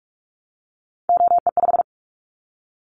Morse code, heard as a keyed tone, about 44 wpm, 700 Hz. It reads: OE5
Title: Callsign Prefixes for Top 25 DXCC - Repeated 3x - 45wpm